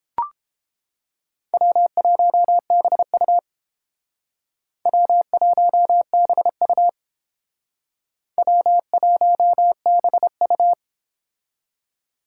Morse code, heard as a keyed tone, about 28 wpm, 700 Hz. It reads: W1BU W1BU W1BU